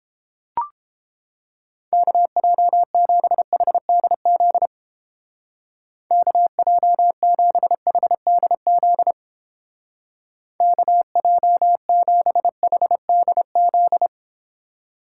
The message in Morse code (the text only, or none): KJ7HDZ KJ7HDZ KJ7HDZ